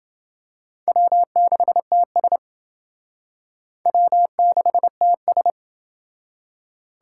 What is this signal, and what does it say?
Morse code, heard as a keyed tone, about 29 wpm, 700 Hz. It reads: W6TS W6TS